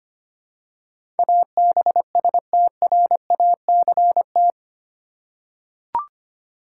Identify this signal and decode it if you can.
Morse code, heard as a keyed tone, about 25 wpm, 700 Hz. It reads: ABSTRACT
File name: 8 Letter Words - First 3 Letters Given - 25wpm